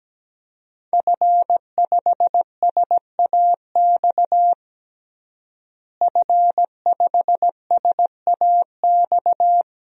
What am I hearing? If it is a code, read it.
Morse code, heard as a keyed tone, about 17 wpm, 700 Hz. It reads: F5SAX F5SAX
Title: DXCC-100 Call Signs - 5 Characters - Repeated 3x - 17wpm